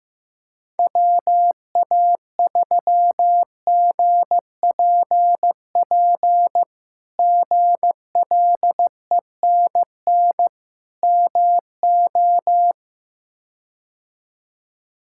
Morse code, heard as a keyed tone, about 16 wpm, 700 Hz. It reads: WA3GPP GLENN MO
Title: K1USN SST QSO Practice - 15wpm